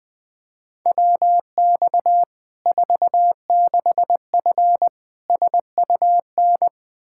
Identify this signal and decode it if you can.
Morse code, heard as a keyed tone, about 20 wpm, 700 Hz. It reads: WX 46F SUN
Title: Practice QSO - Exchange 2 - Info - 20wpm